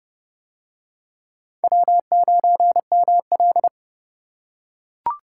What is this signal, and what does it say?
Morse code, heard as a keyed tone, about 29 wpm, 700 Hz. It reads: W9ML